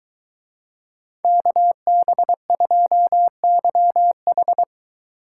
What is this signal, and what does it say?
Morse code, heard as a keyed tone, about 23 wpm, 700 Hz. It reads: KB2YH